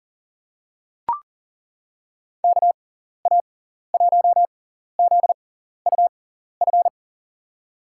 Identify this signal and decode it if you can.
Morse code, heard as a keyed tone, about 40 wpm, spaced as for 14 wpm, 700 Hz. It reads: KA1ZUF
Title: USA Call Signs - 6 Characters - 6x Character Spacing - 40wpm